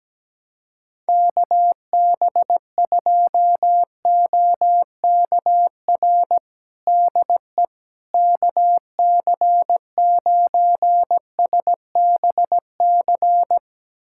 Morse code, heard as a keyed tone, about 17 wpm, 700 Hz. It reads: KB2OKR DE KC9SBC
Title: USA Callsign DE Callsign - 6 Characters - 17wpm